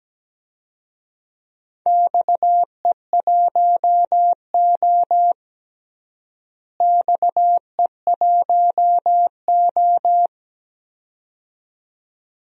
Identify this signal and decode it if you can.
Morse code, heard as a keyed tone, about 17 wpm, 700 Hz. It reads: XE1O XE1O
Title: DXCC-100 Call Signs - 4 Characters - Repeated 2x - 17wpm